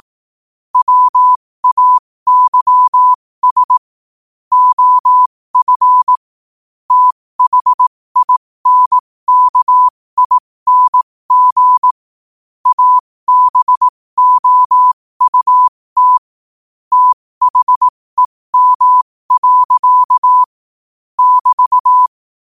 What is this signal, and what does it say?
Morse code, heard as a keyed tone, about 18 wpm, spaced as for 15 wpm, 1000 Hz. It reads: WAYS OF THINKING ABOUT THEM. <BT>
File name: qotdfiles / 15wpm